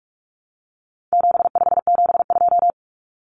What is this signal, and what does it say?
Morse code, heard as a keyed tone, about 44 wpm, 700 Hz. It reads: 7572
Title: Numbers - 4-Digit 45wpm